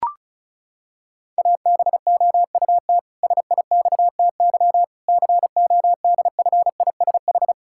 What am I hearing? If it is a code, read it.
Morse code, heard as a keyed tone, about 36 wpm, 700 Hz. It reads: ABOUT SIXTY CODFISH
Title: Pangram Challenge v2 - 35wpm